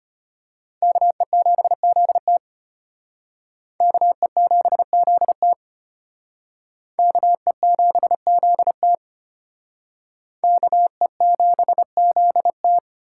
Morse code, heard as a keyed tone, about 31 wpm, 700 Hz. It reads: KE7ZT KE7ZT KE7ZT KE7ZT
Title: Speed Racing - USA 2x2 Call Signs 25wpm